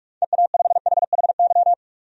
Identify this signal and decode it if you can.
Morse code, heard as a keyed tone, about 47 wpm, 700 Hz. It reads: EA5HHY
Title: DXCC-100 Call Signs - 6 Characters - 45wpm